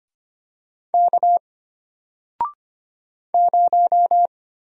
Morse code, heard as a keyed tone, about 25 wpm, 700 Hz. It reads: K 0
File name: Single-Letter-Number-25wpm